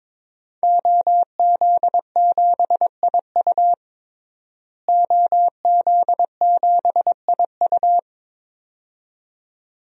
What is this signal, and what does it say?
Morse code, heard as a keyed tone, about 21 wpm, 700 Hz. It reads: OZ7IU OZ7IU
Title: DXCC-100 Call Signs - 5 Characters - Repeated 2x - 22wpm